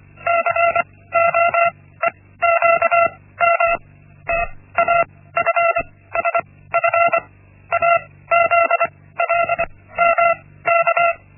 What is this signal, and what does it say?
Morse code, heard as a keyed tone, about 25 wpm, spaced as for 16 wpm, 680 Hz. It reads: COEQM TAFSF AZLMK
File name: Linksmakalnis-1